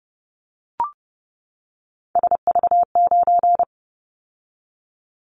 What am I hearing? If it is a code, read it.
Morse code, heard as a keyed tone, about 30 wpm, 700 Hz. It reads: SV9
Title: Callsign Prefixes for Top 100 DXCC - 30wpm